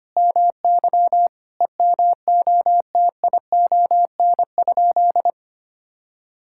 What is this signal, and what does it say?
Morse code, heard as a keyed tone, about 25 wpm, 700 Hz. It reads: MY EMOTION?
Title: Sentences from Top 1600 Words 25wpm